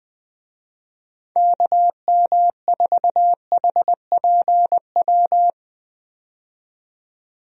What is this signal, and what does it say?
Morse code, heard as a keyed tone, about 20 wpm, 700 Hz. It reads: KM4HPW
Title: USA Call Signs - 6 Characters - 20wpm